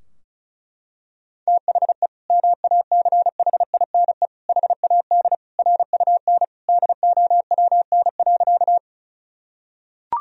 Morse code, heard as a keyed tone, about 36 wpm, 700 Hz. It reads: THE MACHINE HAD RUN DOWN.
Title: Sentences from Top 1000 Words - Review Part 2 - 35wpm